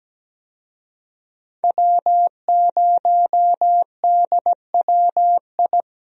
Morse code, heard as a keyed tone, about 17 wpm, 700 Hz. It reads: W0DWI
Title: USA Call Signs - 5 Characters - Repeated 2x - 17wpm